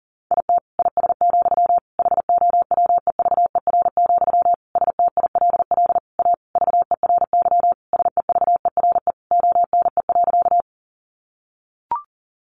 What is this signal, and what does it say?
Morse code, heard as a keyed tone, about 40 wpm, 700 Hz. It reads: IT IS, HOWEVER, STILL A VERY SEVERE ONE.
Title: Sentences from Top 1400 Words 40wpm